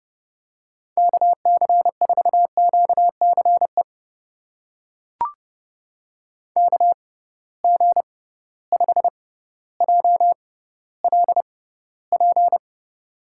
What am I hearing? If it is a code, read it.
Morse code, heard as a keyed tone, about 30 wpm, spaced as for 15 wpm, 700 Hz. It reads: KC4QCE KG5JLP
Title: USA Call Signs - 6 Characters - 6x Character Spacing - 30wpm